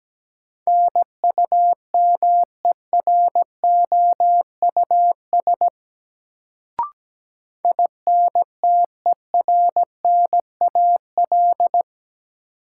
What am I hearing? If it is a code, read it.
Morse code, heard as a keyed tone, about 17 wpm, 700 Hz. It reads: NUMEROUS INTERNAL
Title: Sentences from Top 1400 Words - New Word Drill - 17wpm